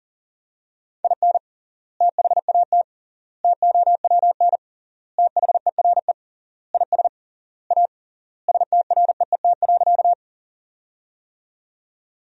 Morse code, heard as a keyed tone, about 40 wpm, 700 Hz. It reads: IN THAT TOWN THERE IS A STREET.
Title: 3x Word-Spacing - Sentences from Top 500 Words - 40wpm